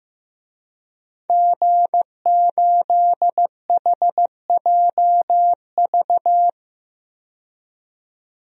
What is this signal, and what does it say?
Morse code, heard as a keyed tone, about 15 wpm, 700 Hz. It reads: G8HJV
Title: DXCC-100 Call Signs - 5 Characters - 15wpm